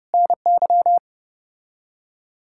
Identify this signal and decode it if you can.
Morse code, heard as a keyed tone, about 33 wpm, 700 Hz. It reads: NY